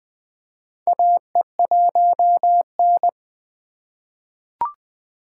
Morse code, heard as a keyed tone, about 20 wpm, 700 Hz. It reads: AE1N
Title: K1USN SST QSO Practice - 20wpm